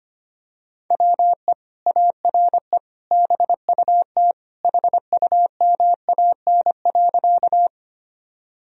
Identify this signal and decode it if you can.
Morse code, heard as a keyed tone, about 25 wpm, 700 Hz. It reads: WE ARE BUT HUMAN.
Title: Sentences from Top 1000 Words - Review Part 3 - 25wpm